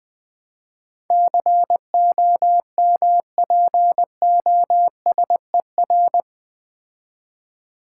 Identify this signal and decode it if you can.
Morse code, heard as a keyed tone, about 21 wpm, 700 Hz. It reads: COMPOSER
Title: Build A Word - 3 Letters At A Time - 8-Letter Words - 20wpm